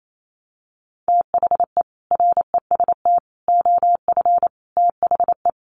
Morse code, heard as a keyed tone, about 28 wpm, 700 Hz. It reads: THE REST OF THE
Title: Sets of 5 Words 28wpm